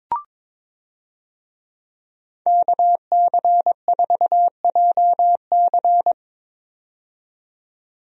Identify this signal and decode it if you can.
Morse code, heard as a keyed tone, about 22 wpm, 700 Hz. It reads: KC4JC
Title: USA Call Signs - 5 Characters - 22wpm